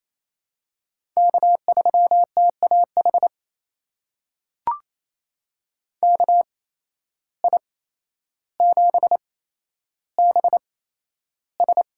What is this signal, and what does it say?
Morse code, heard as a keyed tone, about 28 wpm, spaced as for 12 wpm, 700 Hz. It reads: K3TAH KI7BS
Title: USA Call Signs - 5 Characters - 8x Character Spacing - 28wpm